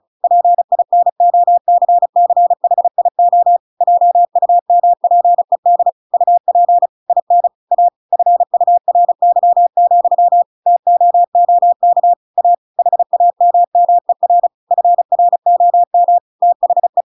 Morse code, heard as a keyed tone, about 35 wpm, 700 Hz. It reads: PINOCCHIO JUMPED UP IN A FURY, TOOK A HAMMER FROM THE